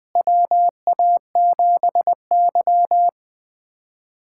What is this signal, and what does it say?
Morse code, heard as a keyed tone, about 20 wpm, 700 Hz. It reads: WA7Y
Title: USA Callsign DE Callsign - 4 Characters - 20wpm